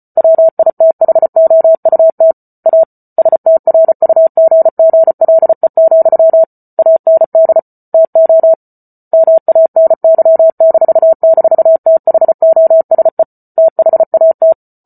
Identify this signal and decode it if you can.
Morse code, heard as a keyed tone, about 34 wpm, 650 Hz. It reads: WITHOUT A STRUGGLE, AND TO MANY--THOSE THAT
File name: Book Two / 08-Chapter 8 - 34 WPM